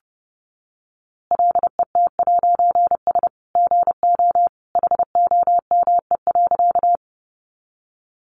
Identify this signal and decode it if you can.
Morse code, heard as a keyed tone, about 30 wpm, 700 Hz. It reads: LET'S GO HOME.
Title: A Sound Of Thunder 30wpm